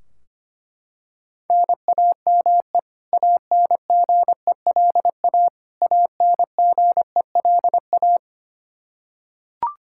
Morse code, heard as a keyed tone, about 25 wpm, 700 Hz. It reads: NAME ANGELA ANGELA
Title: Practice QSO - Exchange 1 - Beginning - 25wpm